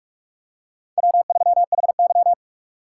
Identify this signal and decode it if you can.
Morse code, heard as a keyed tone, about 45 wpm, 700 Hz. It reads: W3HY
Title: USA Call Signs - 4x Character Spacing - 45wpm